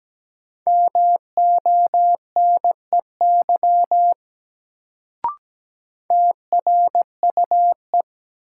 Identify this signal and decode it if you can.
Morse code, heard as a keyed tone, about 17 wpm, 700 Hz. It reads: MONEY TRUE
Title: Sentences from Top 400 Words - New Word Drill - 17wpm